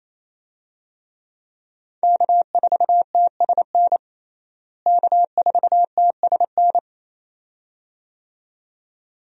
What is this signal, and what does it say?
Morse code, heard as a keyed tone, about 28 wpm, 700 Hz. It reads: K4TSN K4TSN
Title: USA Call Signs - 5 Characters - Repeated 2x - 28wpm